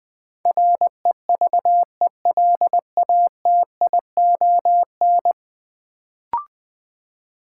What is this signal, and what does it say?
Morse code, heard as a keyed tone, about 20 wpm, 700 Hz. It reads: REVELATION
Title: Build A Word - 5 Letters At A Time - 10-Letter Words - 20wpm